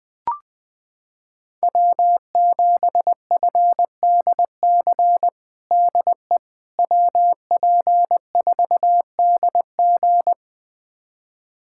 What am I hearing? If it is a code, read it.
Morse code, heard as a keyed tone, about 20 wpm, 700 Hz. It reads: W7FDC DE WP4DG
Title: USA Callsign DE Callsign - 5 Characters - 20wpm